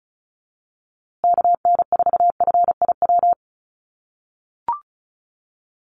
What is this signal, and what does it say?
Morse code, heard as a keyed tone, about 34 wpm, 700 Hz. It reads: KN4FIW
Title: USA Call Signs Repeated 2x 35wpm